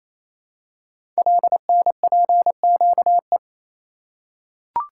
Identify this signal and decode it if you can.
Morse code, heard as a keyed tone, about 28 wpm, 700 Hz. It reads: LNPQE
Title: Code Groups - 5 Letters - 28wpm